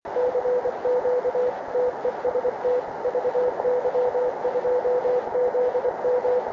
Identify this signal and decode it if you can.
Morse code, heard as a keyed tone, about 26 wpm, 500 Hz, with noise. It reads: CQ TEST VY2ZM
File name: klub / cqww vy2zm